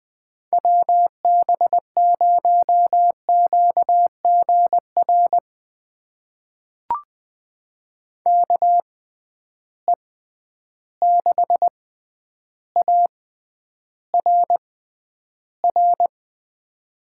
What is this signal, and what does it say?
Morse code, heard as a keyed tone, about 20 wpm, spaced as for 10 wpm, 700 Hz. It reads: WB0QGR KE6ARR
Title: USA Call Signs - 6 Characters - 6x Character Spacing - 20wpm